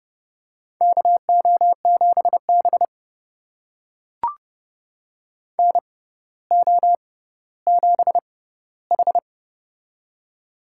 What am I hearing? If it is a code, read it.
Morse code, heard as a keyed tone, about 30 wpm, spaced as for 15 wpm, 700 Hz. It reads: KO7B NO7H
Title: USA Call Signs - 6x Character Spacing - 30wpm